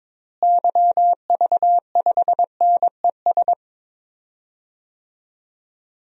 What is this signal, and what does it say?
Morse code, heard as a keyed tone, about 22 wpm, 700 Hz. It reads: YV5NES
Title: DXCC-100 Call Signs - 6 Characters - Repeated 2x - 22wpm